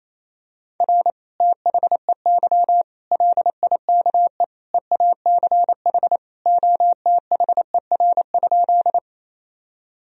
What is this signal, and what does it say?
Morse code, heard as a keyed tone, about 28 wpm, 700 Hz. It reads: R THEY LIKE EACH OTHER?
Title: Sentences from Top 200 Words and Abbreviations 28wpm